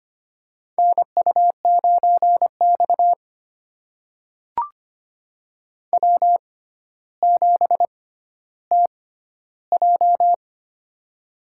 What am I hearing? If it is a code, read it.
Morse code, heard as a keyed tone, about 25 wpm, spaced as for 13 wpm, 700 Hz. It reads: NU9X W7TJ
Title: USA Call Signs - 4 Characters - 6x Character Spacing - 25wpm